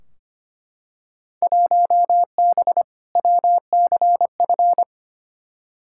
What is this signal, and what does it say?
Morse code, heard as a keyed tone, about 25 wpm, 700 Hz. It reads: 1B WCF
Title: Field Day Exchanges Encore - 25wpm